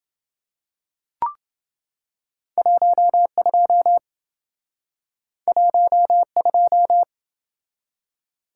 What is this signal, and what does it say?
Morse code, heard as a keyed tone, about 27 wpm, 700 Hz. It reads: 12 12